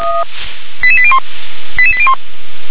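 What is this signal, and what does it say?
Five-tone selective call (ZVEI), heard as quick runs of five tones, tones of about 70 ms. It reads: 88801 88801
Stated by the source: ZVEI five-tone selective call received by SDR.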